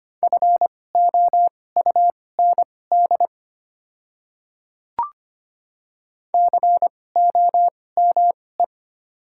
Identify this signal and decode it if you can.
Morse code, heard as a keyed tone, about 25 wpm, spaced as for 18 wpm, 700 Hz. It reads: FOUND COME
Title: 2x Character Spacing - Top 300 Words - 25wpm